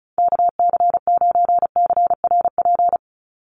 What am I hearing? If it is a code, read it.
Morse code, heard as a keyed tone, about 34 wpm, 700 Hz. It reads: KC9CRP
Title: USA Call Signs 35wpm